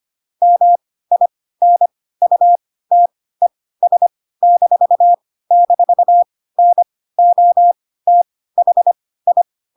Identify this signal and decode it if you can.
Morse code, heard as a keyed tone, about 25 wpm, spaced as for 16 wpm, 700 Hz. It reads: MINUTES--NOTHI
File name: Pinocchio / Pinocchio ch1 10 25 ik1hge